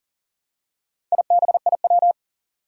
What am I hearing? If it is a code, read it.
Morse code, heard as a keyed tone, about 41 wpm, 700 Hz. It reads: IBIW
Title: Four Challenging Letter-Number Combinations 40wpm